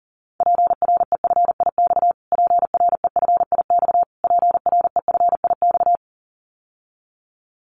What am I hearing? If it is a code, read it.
Morse code, heard as a keyed tone, about 40 wpm, 700 Hz. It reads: PREFIX PREFIX PREFIX
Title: Callsign Prefixes for Top 25 DXCC - Repeated 3x - 40wpm